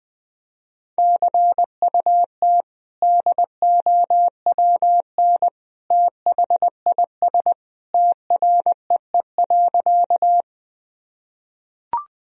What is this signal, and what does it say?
Morse code, heard as a keyed tone, about 19 wpm, 700 Hz. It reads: CUT DOWN THIS TREE.
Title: Sentences from Top 1000 Words - Review Part 1 - 20wpm